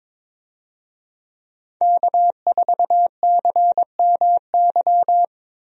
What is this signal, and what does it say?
Morse code, heard as a keyed tone, about 22 wpm, 700 Hz. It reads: K4CMY
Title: USA Call Signs - 5 Characters - Repeated 2x - 22wpm